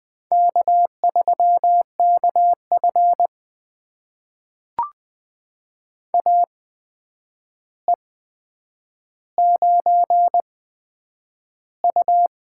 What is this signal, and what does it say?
Morse code, heard as a keyed tone, about 20 wpm, spaced as for 9 wpm, 700 Hz. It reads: K3KF AE9U
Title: USA Call Signs - 4 Characters - 8x Character Spacing - 20wpm